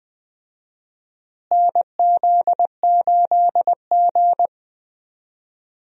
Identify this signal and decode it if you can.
Morse code, heard as a keyed tone, about 20 wpm, 700 Hz. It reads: NZ8G
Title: USA Call Signs - 4 Characters - Repeated 2x - 20wpm